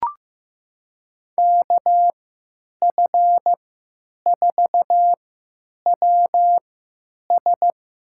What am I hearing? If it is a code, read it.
Morse code, heard as a keyed tone, about 15 wpm, spaced as for 9 wpm, 700 Hz. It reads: KF4WS
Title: USA Call Signs - 3x Character Spacing - 15wpm